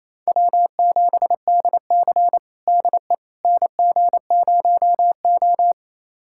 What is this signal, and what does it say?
Morse code, heard as a keyed tone, about 28 wpm, 700 Hz. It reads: W7DC DE NG0O
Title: USA Callsign DE Callsign - 4 Characters - 28wpm